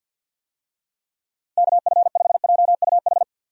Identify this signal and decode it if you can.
Morse code, heard as a keyed tone, about 50 wpm, 700 Hz. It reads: KF5JUH
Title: USA Call Signs - 6 Characters - Repeated 2x - 50wpm